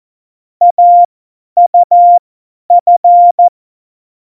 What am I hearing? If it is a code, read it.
Morse code, heard as a keyed tone, about 14 wpm, spaced as for 10 wpm, 700 Hz. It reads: AUF
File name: Morsekurs / Witze / Witz 15. September 2020, 50 F70